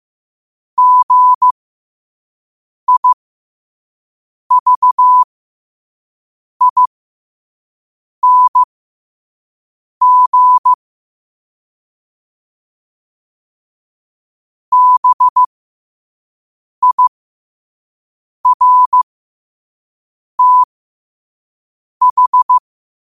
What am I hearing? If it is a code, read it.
Morse code, heard as a keyed tone, about 15 wpm, spaced as for 5 wpm, 1000 Hz. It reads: GIVING BIRTH
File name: qotdfiles / 5wpm